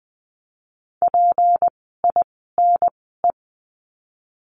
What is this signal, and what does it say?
Morse code, heard as a keyed tone, about 20 wpm, spaced as for 15 wpm, 700 Hz. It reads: PINE
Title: 153 Lesson 25f - Letter p - Words 20wpm